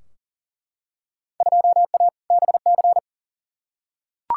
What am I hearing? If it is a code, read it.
Morse code, heard as a keyed tone, about 40 wpm, 700 Hz. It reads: 2A BC
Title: Field Day Exchanges - 40wpm